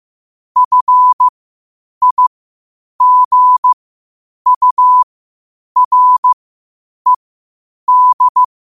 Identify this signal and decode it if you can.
Morse code, heard as a keyed tone, about 15 wpm, spaced as for 8 wpm, 1000 Hz. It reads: FIGURED
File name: qotdfiles / 8wpm